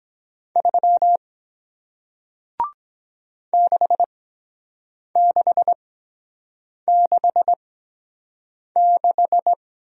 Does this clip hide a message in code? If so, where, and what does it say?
Yes: Morse code from 0.6 to 9.5 s, heard as a keyed tone, about 23 wpm, spaced as for 7 wpm, 700 Hz. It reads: 3 6666